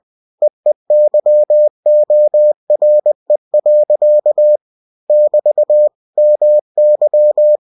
Morse code, heard as a keyed tone, about 20 wpm, 600 Hz. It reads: EEYORE. <BT> MY
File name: cw-books / winnie-the-pooh / winnie-20wpm-page-0054